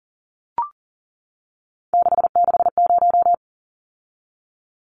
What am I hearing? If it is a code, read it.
Morse code, heard as a keyed tone, about 40 wpm, 700 Hz. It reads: 660
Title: Numbers - 3-Digit 40wpm